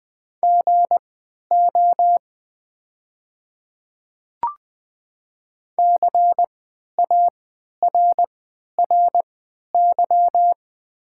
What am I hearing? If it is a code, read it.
Morse code, heard as a keyed tone, about 20 wpm, spaced as for 11 wpm, 700 Hz. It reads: GO CARRY
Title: 3x Character Spacing - Top 500 Words - 20wpm